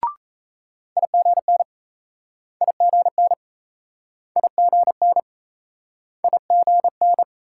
Morse code, heard as a keyed tone, about 36 wpm, 700 Hz. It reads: IGN IGN IGN IGN